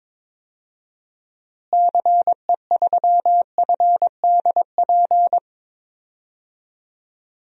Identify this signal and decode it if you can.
Morse code, heard as a keyed tone, about 22 wpm, 700 Hz. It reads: CE3FDP
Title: DXCC-100 Call Signs - 6 Characters - Repeated 2x - 22wpm